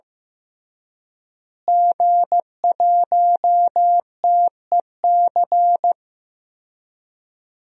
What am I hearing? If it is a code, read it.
Morse code, heard as a keyed tone, about 14 wpm, 700 Hz. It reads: G1TEC
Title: DXCC-100 Call Signs - 5 Characters - Repeated 3x - 15wpm